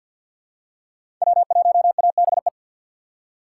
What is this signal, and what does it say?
Morse code, heard as a keyed tone, about 48 wpm, 700 Hz. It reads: W1ABE
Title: USA Call Signs - 5 Characters - 3x Character Spacing - 50wpm